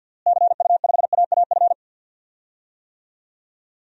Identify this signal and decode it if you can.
Morse code, heard as a keyed tone, about 50 wpm, 700 Hz. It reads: CU5AAF
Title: DXCC-100 Call Signs - 6 Characters - Repeated 3x - 50wpm